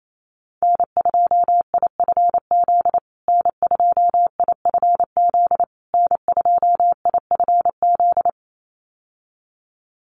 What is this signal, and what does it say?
Morse code, heard as a keyed tone, about 28 wpm, 700 Hz. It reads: N2IFZ N2IFZ N2IFZ
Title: USA Call Signs Repeated 3x 28wpm